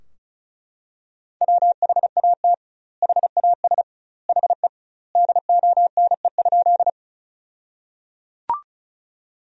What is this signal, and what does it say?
Morse code, heard as a keyed tone, about 36 wpm, 700 Hz. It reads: WHAT HAS HE DONE?
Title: 2x Word-Spacing - Sentences from Top 500 Words - 35wpm